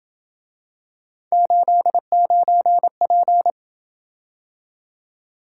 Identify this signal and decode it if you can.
Morse code, heard as a keyed tone, about 27 wpm, 700 Hz. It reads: 89P